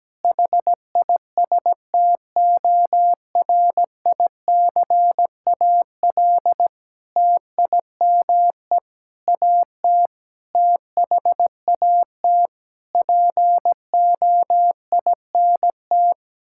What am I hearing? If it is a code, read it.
Morse code, heard as a keyed tone, about 17 wpm, 700 Hz. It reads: HISTORICAL TIME AT THAT POINT
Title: Sentences from Top 1000 Words - Review Part 5 - 17wpm